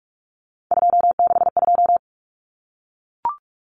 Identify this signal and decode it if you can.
Morse code, heard as a keyed tone, about 44 wpm, 700 Hz. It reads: 262
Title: Numbers - 3-Digit 45wpm